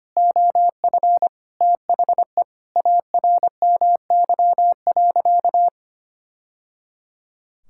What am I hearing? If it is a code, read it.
Morse code, heard as a keyed tone, about 25 wpm, 700 Hz. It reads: OF THE ARMY.
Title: Sentences from Top 1000 Words - Review Part 6 - 25wpm